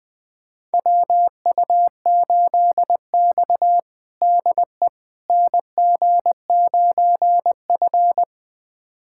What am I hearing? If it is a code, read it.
Morse code, heard as a keyed tone, about 21 wpm, 700 Hz. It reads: WU8X DE NG9F
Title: USA Callsign DE Callsign - 4 Characters - 20wpm